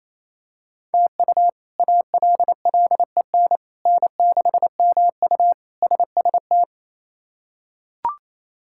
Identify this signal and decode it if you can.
Morse code, heard as a keyed tone, about 28 wpm, 700 Hz. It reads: TU ALLEN N6MU SST
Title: K1USN SST QSO Practice - 28wpm